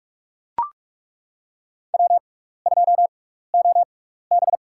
Morse code, heard as a keyed tone, about 42 wpm, spaced as for 15 wpm, 700 Hz. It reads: W2OB
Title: USA Call Signs - 6x Character Spacing - 45wpm